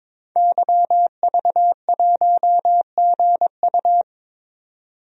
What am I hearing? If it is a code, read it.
Morse code, heard as a keyed tone, about 22 wpm, 700 Hz. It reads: YV1GU
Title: DXCC-100 Call Signs - 5 Characters - 22wpm